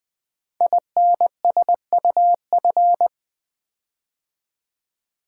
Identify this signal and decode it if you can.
Morse code, heard as a keyed tone, about 20 wpm, 700 Hz. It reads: INSUF
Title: Build A Word - 5 Letters At A Time - 12-Letter Words - 20wpm